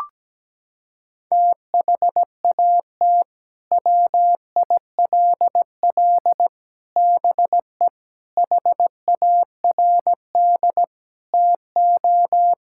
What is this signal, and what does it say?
Morse code, heard as a keyed tone, about 17 wpm, 700 Hz. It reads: THAT WILL BE HARD TO
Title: Sentences from Top 1000 Words - Review Part 5 - 17wpm